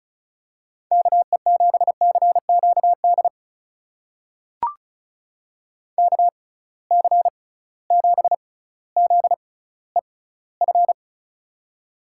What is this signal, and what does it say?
Morse code, heard as a keyed tone, about 36 wpm, spaced as for 18 wpm, 700 Hz. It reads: KE7CQD KC7ZEF
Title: USA Call Signs - 6 Characters - 6x Character Spacing - 35wpm